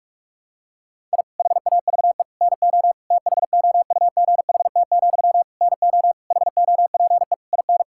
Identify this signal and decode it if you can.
Morse code, heard as a keyed tone, about 46 wpm, 700 Hz. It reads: I HAVE NO THOUGHT, NO HOPE IN
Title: Sentences from Top 1000 Words - Review Part 6 - 45wpm